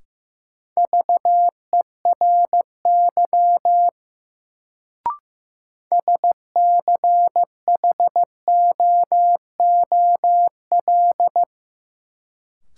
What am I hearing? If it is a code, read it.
Morse code, heard as a keyed tone, about 15 wpm, 700 Hz. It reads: VERY SCHOOL
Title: Sets of 2 Words - New Word Drill - 15wpm